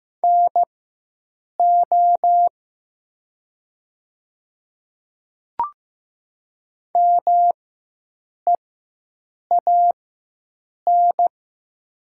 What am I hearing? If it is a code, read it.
Morse code, heard as a keyed tone, about 15 wpm, spaced as for 7 wpm, 700 Hz. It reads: NO MEAN
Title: 4x Character Spacing - Top 300 Words - 15wpm